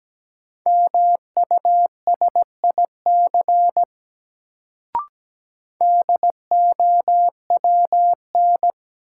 Morse code, heard as a keyed tone, about 17 wpm, 700 Hz. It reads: MUSIC DOWN
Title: Sentences from Top 300 Words - New Word Drill - 17wpm